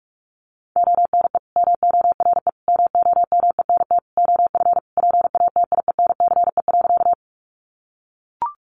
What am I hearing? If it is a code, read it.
Morse code, heard as a keyed tone, about 44 wpm, 700 Hz. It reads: ONE MORE MOMENT OF PATIENCE.
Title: Sentences from Top 1300 Words 45wpm